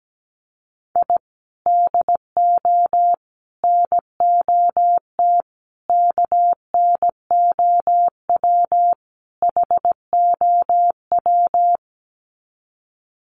Morse code, heard as a keyed tone, about 17 wpm, 700 Hz. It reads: I DO NOT KNOW HOW
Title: Sets of 5 Words 17wpm